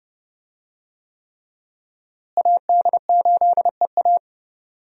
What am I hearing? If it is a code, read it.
Morse code, heard as a keyed tone, about 30 wpm, 700 Hz. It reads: AD8EA
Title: USA Call Signs - 5 Characters - Repeated 3x - 30wpm